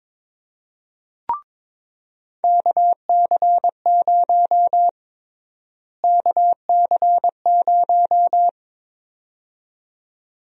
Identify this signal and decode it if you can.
Morse code, heard as a keyed tone, about 21 wpm, 700 Hz. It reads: KC0 KC0
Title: USA Call Sign Prefixes - Repeated 2x - 22wpm